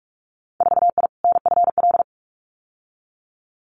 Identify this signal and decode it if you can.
Morse code, heard as a keyed tone, about 44 wpm, 700 Hz. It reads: 4I NFL
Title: Winter Field Day Exchanges 45wpm